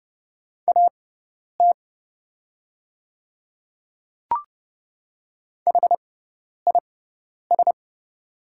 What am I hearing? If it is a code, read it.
Morse code, heard as a keyed tone, about 30 wpm, spaced as for 10 wpm, 700 Hz. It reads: AT HIS
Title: 6x Character Spacing - Top 200 Words - 30wpm